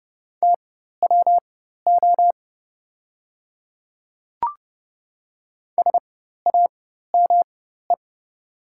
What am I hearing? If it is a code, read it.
Morse code, heard as a keyed tone, about 30 wpm, spaced as for 14 wpm, 700 Hz. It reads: TWO SAME
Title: 4x Character Spacing - Top 300 Words - 30wpm